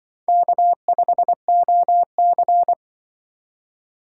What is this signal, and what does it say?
Morse code, heard as a keyed tone, about 24 wpm, 700 Hz. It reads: K5OC
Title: Speed Racing - USA 1x2 Call Signs 20wpm